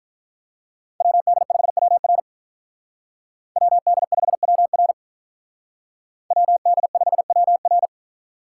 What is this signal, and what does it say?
Morse code, heard as a keyed tone, about 45 wpm, 700 Hz. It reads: WD5WR WD5WR WD5WR